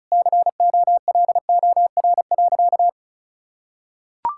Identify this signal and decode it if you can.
Morse code, heard as a keyed tone, about 34 wpm, 700 Hz. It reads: COLOR.
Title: Sentences from Top 1600 Words 35wpm